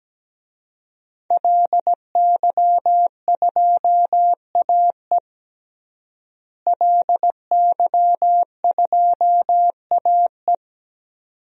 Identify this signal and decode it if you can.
Morse code, heard as a keyed tone, about 17 wpm, 700 Hz. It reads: LY2AE LY2AE
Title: DXCC-100 Call Signs - 5 Characters - Repeated 2x - 17wpm